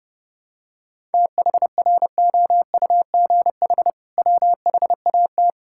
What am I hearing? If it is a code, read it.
Morse code, heard as a keyed tone, about 30 wpm, 700 Hz. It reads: THROUGH WHAT
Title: Sentences from Top 1000 Words - Review Part 6 - 30wpm